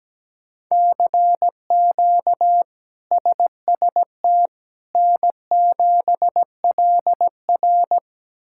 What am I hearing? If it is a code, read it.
Morse code, heard as a keyed tone, about 17 wpm, 700 Hz. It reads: CQ SST N7LR
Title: K1USN SST QSO Practice - 17wpm